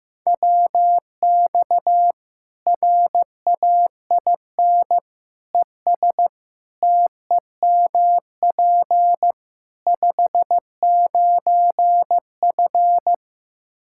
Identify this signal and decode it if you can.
Morse code, heard as a keyed tone, about 15 wpm, 700 Hz. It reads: WX RAIN ES TEMP 59F
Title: Practice QSO - Exchange 2 - Info - 15wpm